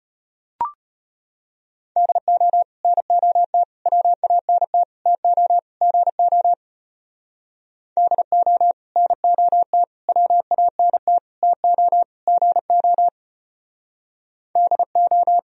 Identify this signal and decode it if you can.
Morse code, heard as a keyed tone, about 31 wpm, 700 Hz. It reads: DO NOT WANT TO GO DO NOT WANT TO GO DO
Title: Speed-Racing - Sets of 5 Words 25wpm